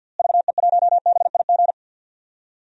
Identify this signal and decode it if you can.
Morse code, heard as a keyed tone, about 49 wpm, 700 Hz. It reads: VE1BIG
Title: DXCC-100 Call Signs - 6 Characters - Repeated 3x - 50wpm